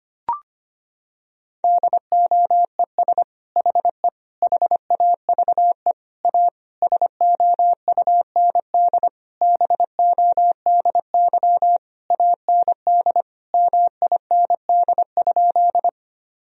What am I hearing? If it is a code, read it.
Morse code, heard as a keyed tone, about 25 wpm, 700 Hz. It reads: DOES HE HAVE A SOUND BODY AND MIND?
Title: Sentences from Top 1000 Words - Review Part 2 - 25wpm